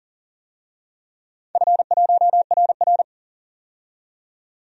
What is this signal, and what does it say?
Morse code, heard as a keyed tone, about 40 wpm, 700 Hz. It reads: F1RR
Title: DXCC-100 Call Signs - 4 Characters - 40wpm